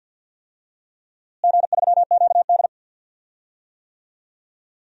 Morse code, heard as a keyed tone, about 46 wpm, 700 Hz. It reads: G3QD
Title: DXCC-100 Call Signs - 4 Characters - Repeated 3x - 50wpm